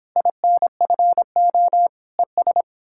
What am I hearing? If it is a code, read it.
Morse code, heard as a keyed tone, about 29 wpm, 700 Hz. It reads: INFO ES